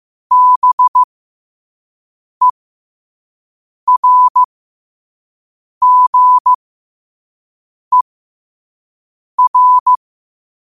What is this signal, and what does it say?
Morse code, heard as a keyed tone, about 15 wpm, spaced as for 5 wpm, 1000 Hz. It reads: BERGER
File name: qotdfiles / 5wpm